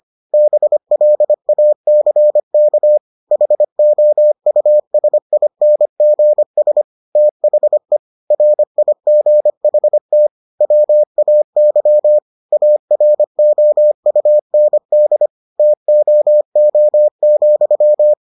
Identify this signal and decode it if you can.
Morse code, heard as a keyed tone, about 25 wpm, 600 Hz. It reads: BLACK HOUSINGS THE RIGHT WAY AROUND TOO,